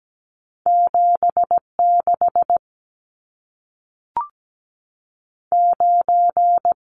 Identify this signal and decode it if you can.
Morse code, heard as a keyed tone, about 17 wpm, 700 Hz. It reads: 7 6 9
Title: Numbers - 2-Digit 17wpm